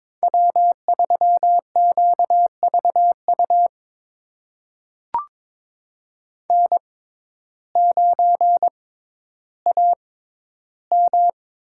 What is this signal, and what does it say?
Morse code, heard as a keyed tone, about 22 wpm, spaced as for 12 wpm, 700 Hz. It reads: W3QVU N9AM
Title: USA Call Signs - 5 Characters - 6x Character Spacing - 22wpm